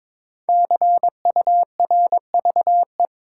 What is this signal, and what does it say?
Morse code, heard as a keyed tone, about 22 wpm, 700 Hz. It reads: CURVE
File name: 5 Letter Words - First 3 Letters Given - 22wpm